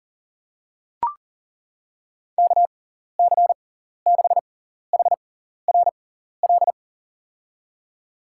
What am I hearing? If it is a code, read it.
Morse code, heard as a keyed tone, about 40 wpm, spaced as for 14 wpm, 700 Hz. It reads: KC6HRL
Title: USA Call Signs - 6 Characters - 6x Character Spacing - 40wpm